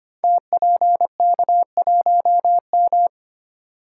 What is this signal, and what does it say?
Morse code, heard as a keyed tone, about 25 wpm, 700 Hz. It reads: TPK1M
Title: Code Groups - 5 Letter:Numbers - 25wpm